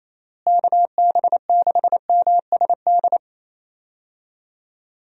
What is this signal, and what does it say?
Morse code, heard as a keyed tone, about 28 wpm, 700 Hz. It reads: KB6MSD
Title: USA Call Signs - 6 Characters - Repeated 3x - 28wpm